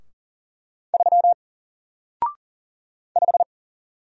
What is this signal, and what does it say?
Morse code, heard as a keyed tone, about 40 wpm, 700 Hz. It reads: 3 5
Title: Single Character (Letter:Number:Punctuation:Prosign) Alt 40wpm